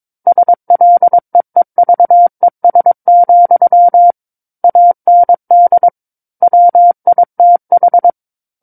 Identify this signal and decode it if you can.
Morse code, heard as a keyed tone, about 22 wpm, 700 Hz. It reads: SLEEVES, AND WITH
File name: Book One / 13-Chapter 13 - 22 WPM